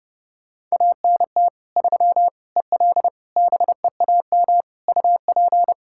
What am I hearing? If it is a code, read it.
Morse code, heard as a keyed tone, about 32 wpm, 700 Hz. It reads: ANT 3 EL BEAM UP
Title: Practice QSO - Exchange 2 - Info - 30wpm